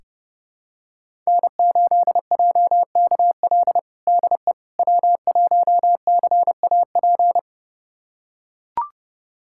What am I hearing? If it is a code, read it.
Morse code, heard as a keyed tone, about 30 wpm, 700 Hz. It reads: N8JKL DE W1CAP
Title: USA Callsign DE Callsign - 5 Characters - 30wpm